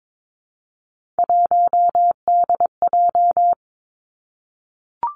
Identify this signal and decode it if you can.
Morse code, heard as a keyed tone, about 22 wpm, 700 Hz. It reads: 1DJ
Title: Three Challenging Letter-Number Combinations 22wpm